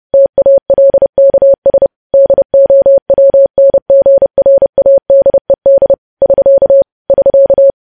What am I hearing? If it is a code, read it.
Morse code, heard as a keyed tone, about 30 wpm, 550 Hz. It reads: TALKS DOWNGRADED <SK> <SK>
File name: podcast / News headlines-1x-30x30